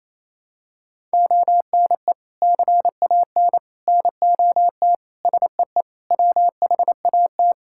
Morse code, heard as a keyed tone, about 28 wpm, 700 Hz. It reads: ONE CAN NOT SEE WHAT
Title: Sentences from Top 1000 Words - Review Part 5 - 28wpm